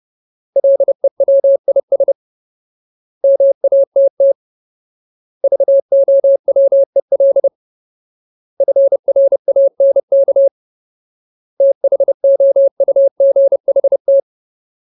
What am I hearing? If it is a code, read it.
Morse code, heard as a keyed tone, about 30 wpm, 550 Hz. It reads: LEWIS MATT VOWEL FRANK THOUGHT
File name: Ultimate Word List / 30WPM Ultimate